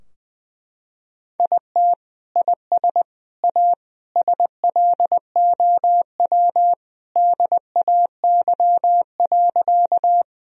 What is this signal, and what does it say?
Morse code, heard as a keyed tone, about 21 wpm, 700 Hz. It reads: IT IS A SLOW DAY.
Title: Sentences from Top 1000 Words - Review Part 2 - 20wpm